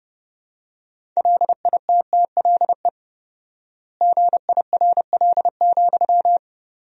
Morse code, heard as a keyed tone, about 32 wpm, 700 Hz. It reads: LITTLE GIRL,
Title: 4x Word-Spacing - Sentences from Top 400 Words - 30wpm